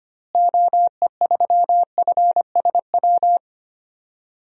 Morse code, heard as a keyed tone, about 25 wpm, 690 Hz. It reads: OE3FSW
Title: DXCC-100 Call Signs - 6 Characters - Repeated 3x - 25wpm